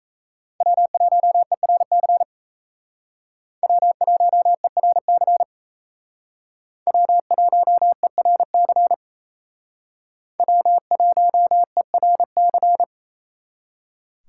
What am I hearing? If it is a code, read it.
Morse code, heard as a keyed tone, about 35 wpm, 700 Hz. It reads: W1ERC W1ERC W1ERC W1ERC